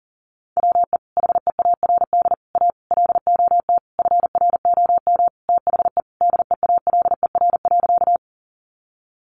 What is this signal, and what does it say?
Morse code, heard as a keyed tone, about 40 wpm, 700 Hz. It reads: WE HEARD A LOT FROM THE DEALER.
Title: Sentences from Top 1100 Words 40wpm